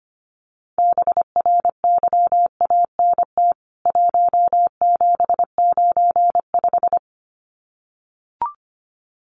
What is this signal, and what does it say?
Morse code, heard as a keyed tone, about 25 wpm, 700 Hz. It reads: BRYANT 1795
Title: CWT Exchange 25wpm